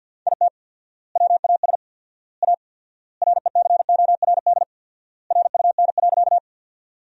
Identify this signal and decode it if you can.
Morse code, heard as a keyed tone, about 50 wpm, 700 Hz. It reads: IT WAS A RECORD RUN.
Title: 4x Word-Spacing - Sentences from Top 500 Words - 50wpm